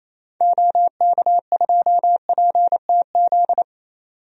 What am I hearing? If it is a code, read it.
Morse code, heard as a keyed tone, about 28 wpm, 700 Hz. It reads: OK2PTZ
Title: DXCC-100 Call Signs - 6 Characters - 28wpm